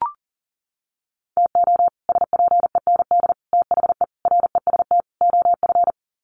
Morse code, heard as a keyed tone, about 37 wpm, 700 Hz. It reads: TO SPEND THE REST OF
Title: Sets of 5 Words Encore2 40wpm